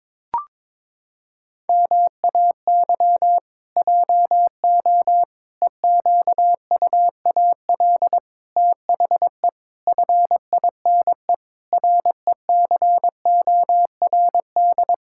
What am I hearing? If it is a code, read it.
Morse code, heard as a keyed tone, about 22 wpm, 700 Hz. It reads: MAY JO EQUAL THE FINE RECORD
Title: Pangram Challenge v2 - 22wpm